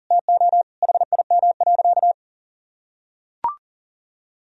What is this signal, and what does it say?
Morse code, heard as a keyed tone, about 40 wpm, 700 Hz. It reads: TO HIM.
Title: Sentences from Top 1000 Words - Review Part 2 - 40wpm